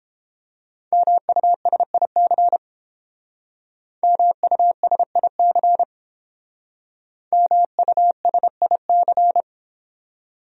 Morse code, heard as a keyed tone, about 31 wpm, 700 Hz. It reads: MUSIC MUSIC MUSIC